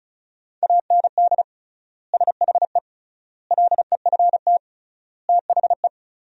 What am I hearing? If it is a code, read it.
Morse code, heard as a keyed tone, about 36 wpm, 700 Hz. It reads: AND SHE LEFT THE
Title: 3x Word-Spacing - Sentences from Top 300 Words - 35wpm